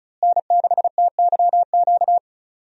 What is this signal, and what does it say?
Morse code, heard as a keyed tone, about 36 wpm, 700 Hz. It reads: N6TYQ
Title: USA Call Signs - 5 Characters - Repeated 2x - 35wpm